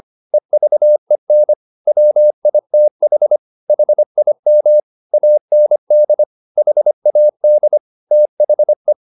Morse code, heard as a keyed tone, about 25 wpm, 600 Hz. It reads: EVEN WITH HIM AND HAD THE